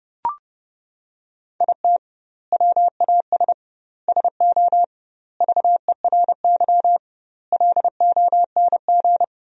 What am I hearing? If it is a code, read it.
Morse code, heard as a keyed tone, about 30 wpm, 700 Hz. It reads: IT WAS SO VERY LONG
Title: 2x Word-Spacing - Sentences from Top 500 Words - 30wpm